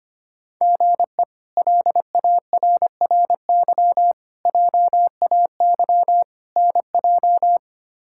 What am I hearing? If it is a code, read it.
Morse code, heard as a keyed tone, about 25 wpm, 700 Hz. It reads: GE LARRY JAY NJ
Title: K1USN SST QSO Practice - 25wpm